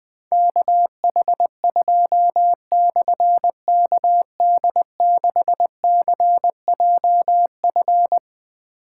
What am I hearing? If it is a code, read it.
Morse code, heard as a keyed tone, about 21 wpm, 700 Hz. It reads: KH2/KD6CJF
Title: Difficult Call Signs 20wpm